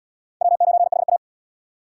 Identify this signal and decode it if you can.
Morse code, heard as a keyed tone, about 65 wpm, 700 Hz. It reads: W9HA